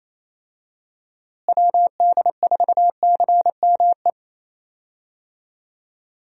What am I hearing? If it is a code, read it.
Morse code, heard as a keyed tone, about 28 wpm, 700 Hz. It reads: WD4CME
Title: USA Call Signs - 6 Characters - Repeated 3x - 28wpm